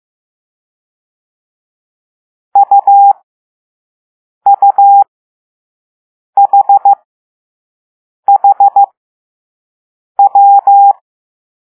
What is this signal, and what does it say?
Morse code, heard as a keyed tone, about 15 wpm, spaced as for 5 wpm, 800 Hz. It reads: UUHHW